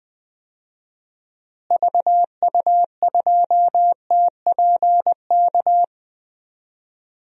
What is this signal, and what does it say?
Morse code, heard as a keyed tone, about 20 wpm, 700 Hz. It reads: VU2TPK
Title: DXCC-100 Call Signs - 6 Characters - Repeated 3x - 20wpm